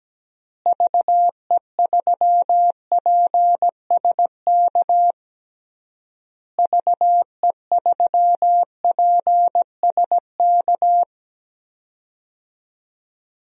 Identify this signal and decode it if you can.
Morse code, heard as a keyed tone, about 17 wpm, 700 Hz. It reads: VE3PSK VE3PSK
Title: DXCC-100 Call Signs - 6 Characters - Repeated 3x - 17wpm